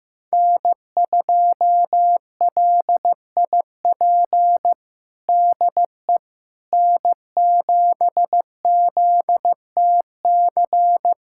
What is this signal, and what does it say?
Morse code, heard as a keyed tone, about 15 wpm, 700 Hz. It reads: N2LIP DE N7ZTC
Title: USA Callsign DE Callsign - 5 Characters - 15wpm